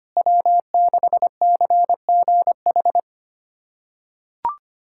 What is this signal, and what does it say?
Morse code, heard as a keyed tone, about 25 wpm, 700 Hz. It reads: W6CGH
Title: USA Callsign DE Callsign - 5 Characters - 25wpm